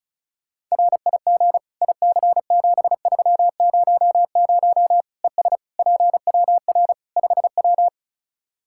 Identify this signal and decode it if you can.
Morse code, heard as a keyed tone, about 36 wpm, 700 Hz. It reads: RIG IC7300 ES PWR 5W
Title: Practice QSO - Exchange 2 - Info - 35wpm